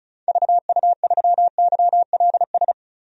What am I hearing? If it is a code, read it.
Morse code, heard as a keyed tone, about 36 wpm, 700 Hz. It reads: VU3YLS
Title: DXCC-100 Call Signs - 6 Characters - 35wpm